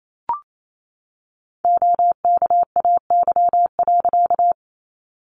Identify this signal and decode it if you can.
Morse code, heard as a keyed tone, about 28 wpm, 700 Hz. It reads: OKAY.
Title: A Sound Of Thunder 28wpm